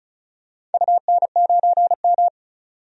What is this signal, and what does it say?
Morse code, heard as a keyed tone, about 36 wpm, 700 Hz. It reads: UN9M
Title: DXCC-100 Call Signs - 4 Characters - 35wpm